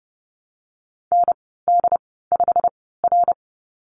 Morse code, heard as a keyed tone, about 32 wpm, spaced as for 17 wpm, 700 Hz. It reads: ND5R
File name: Lesson 19g - Letter w - Call Signs 30wpm